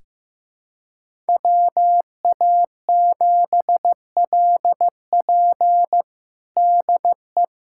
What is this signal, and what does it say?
Morse code, heard as a keyed tone, about 15 wpm, 700 Hz. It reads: WA7LP DE
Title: USA Callsign DE Callsign - 5 Characters - 15wpm